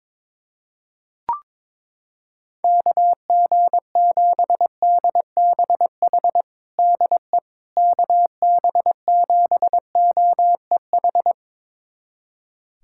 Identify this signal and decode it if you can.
Morse code, heard as a keyed tone, about 21 wpm, 700 Hz. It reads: KG7DBH DE KB7OEH
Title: USA Callsign DE Callsign - 6 Characters - 22wpm